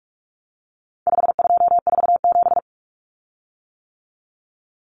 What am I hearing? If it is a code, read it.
Morse code, heard as a keyed tone, about 44 wpm, 700 Hz. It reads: 5247
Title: Common Cut Numbers - 4-Digit 45wpm